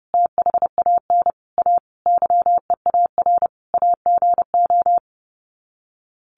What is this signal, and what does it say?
Morse code, heard as a keyed tone, about 30 wpm, 700 Hz. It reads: THAN A YEAR AGO
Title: Sets of 5 Words Encore2 30wpm